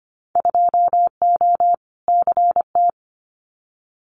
Morse code, heard as a keyed tone, about 25 wpm, 700 Hz. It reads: 2O CT
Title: Winter Field Day Exchanges 25wpm